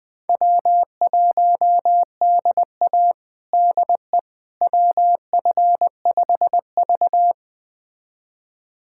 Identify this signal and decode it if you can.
Morse code, heard as a keyed tone, about 20 wpm, 700 Hz. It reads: W1DA DE WF5V
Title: USA Callsign DE Callsign - 4 Characters - 20wpm